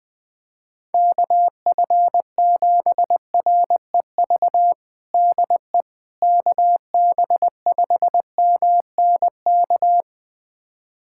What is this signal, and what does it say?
Morse code, heard as a keyed tone, about 20 wpm, 700 Hz. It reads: KF7REV DE KB5MNK
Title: USA Callsign DE Callsign - 6 Characters - 20wpm